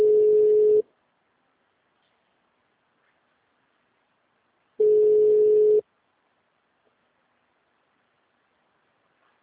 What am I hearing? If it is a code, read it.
Ringback tone (Europe, 425 Hz).
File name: out- -2052- - - .5854